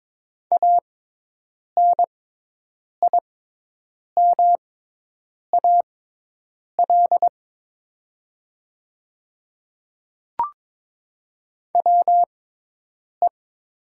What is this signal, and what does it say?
Morse code, heard as a keyed tone, about 22 wpm, spaced as for 8 wpm, 700 Hz. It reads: ANIMAL WE
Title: 6x Character Spacing - Top 200 Words - 22wpm